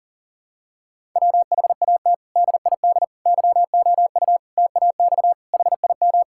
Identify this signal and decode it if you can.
Morse code, heard as a keyed tone, about 40 wpm, 700 Hz. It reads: WHAT DID YOU TAX HIM
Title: Sentences from Top 1000 Words - Review Part 5 - 40wpm